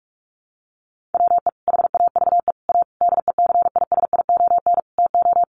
Morse code, heard as a keyed tone, about 46 wpm, 700 Hz. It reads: WE HAVE A DECISION TO
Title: Sentences from Top 1000 Words 45wpm